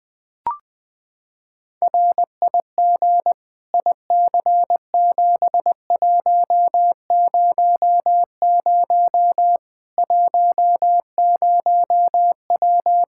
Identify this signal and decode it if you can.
Morse code, heard as a keyed tone, about 20 wpm, 700 Hz. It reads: RIG IC7100 10W
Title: Practice QSO - Exchange 2 - Info - 20wpm